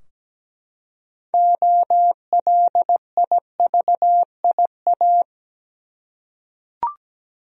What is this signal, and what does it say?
Morse code, heard as a keyed tone, about 17 wpm, 700 Hz. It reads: OLIVIA
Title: USA Common Names - 17wpm